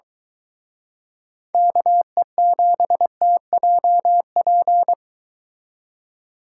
Morse code, heard as a keyed tone, about 22 wpm, 700 Hz. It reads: KE7TJP